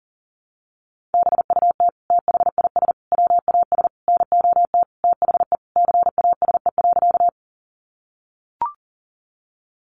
Morse code, heard as a keyed tone, about 40 wpm, 700 Hz. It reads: BUT THIS WAS NOT THE CASE.
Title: Sentences from Top 700 Words 40wpm